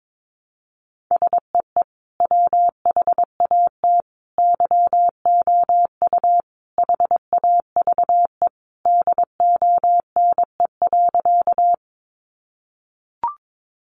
Morse code, heard as a keyed tone, about 22 wpm, 700 Hz. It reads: SEE WHAT YOU HAVE DONE.
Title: Sentences from Top 500 Words 22wpm